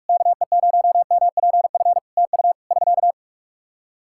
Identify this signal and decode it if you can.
Morse code, heard as a keyed tone, about 46 wpm, 700 Hz. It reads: KE0MPF TU <SK>
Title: Practice QSO - Exchange 4 - Ending - 45wpm